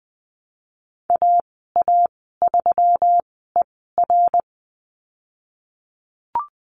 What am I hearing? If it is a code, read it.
Morse code, heard as a keyed tone, about 20 wpm, spaced as for 15 wpm, 700 Hz. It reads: AA3ER
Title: 179 Lesson 29g - Slash - Call Signs 20wpm